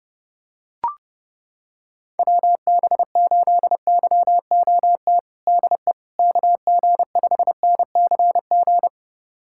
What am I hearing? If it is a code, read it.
Morse code, heard as a keyed tone, about 30 wpm, 700 Hz. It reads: WB8YOT DE KG5NCG
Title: USA Callsign DE Callsign 30wpm